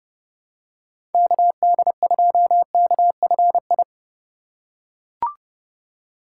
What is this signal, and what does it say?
Morse code, heard as a keyed tone, about 30 wpm, 700 Hz. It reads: KD2KFI